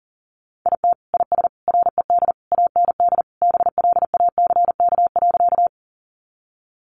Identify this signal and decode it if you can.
Morse code, heard as a keyed tone, about 40 wpm, 700 Hz. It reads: IT IS RED AND BLACK.
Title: Additional Easy Phrases 40wpm